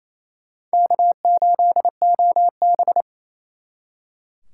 Morse code, heard as a keyed tone, about 35 wpm, 700 Hz. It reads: K8OB